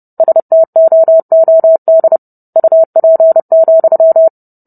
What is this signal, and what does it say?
Morse code, heard as a keyed tone, about 30 wpm, 650 Hz. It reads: STOOD UP,